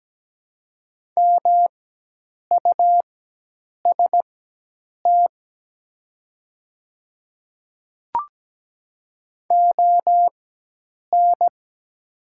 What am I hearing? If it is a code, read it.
Morse code, heard as a keyed tone, about 17 wpm, spaced as for 8 wpm, 700 Hz. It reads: MUST ON
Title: 4x Character Spacing - Top 200 Words - 17wpm